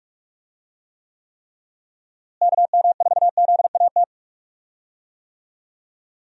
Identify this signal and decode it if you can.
Morse code, heard as a keyed tone, about 44 wpm, 700 Hz. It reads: KM4ZAT
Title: USA Call Signs - 6 Characters - Repeated 3x - 45wpm